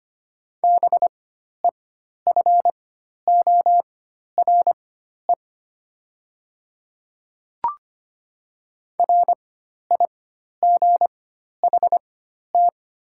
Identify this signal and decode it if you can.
Morse code, heard as a keyed tone, about 25 wpm, spaced as for 12 wpm, 700 Hz. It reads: BEFORE RIGHT
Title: 4x Character Spacing - Top 200 Words - 25wpm